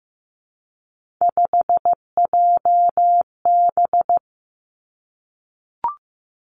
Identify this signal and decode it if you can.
Morse code, heard as a keyed tone, about 15 wpm, 700 Hz. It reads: 5JB
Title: Three Challenging Letter-Number Combinations 15wpm